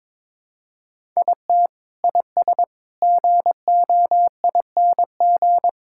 Morse code, heard as a keyed tone, about 22 wpm, 700 Hz. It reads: IT IS GOING
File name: practice / 22wpm / Sentences with 4-Word Phrases - 22wpm